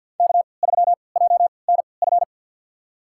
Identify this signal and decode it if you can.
Morse code, heard as a keyed tone, about 51 wpm, spaced as for 28 wpm, 700 Hz. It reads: K3JNF
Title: USA Call Signs - 5 Characters - 3x Character Spacing - 50wpm